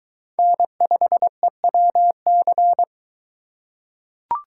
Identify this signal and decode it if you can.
Morse code, heard as a keyed tone, about 23 wpm, 700 Hz. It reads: N5EWC